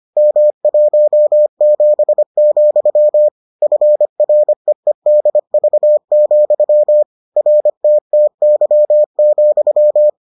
Morse code, heard as a keyed tone, about 25 wpm, 600 Hz. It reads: M17, FREEDV, RTTY,